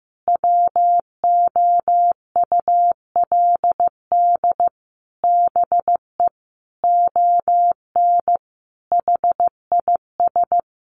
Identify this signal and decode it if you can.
Morse code, heard as a keyed tone, about 15 wpm, 700 Hz. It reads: WOULD BE ON HIS
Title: Sentences from Top 300 Words 15wpm